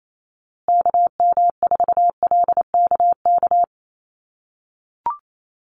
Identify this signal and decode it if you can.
Morse code, heard as a keyed tone, about 28 wpm, 700 Hz. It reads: KM4LKK
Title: USA Call Signs Repeated 2x 28wpm